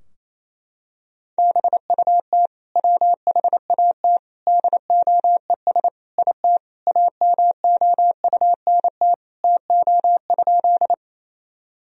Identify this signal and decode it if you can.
Morse code, heard as a keyed tone, about 28 wpm, 700 Hz. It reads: BUT WHAT DOES IT AMOUNT TO?
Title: Sentences from Top 1000 Words - Review Part 4 - 28wpm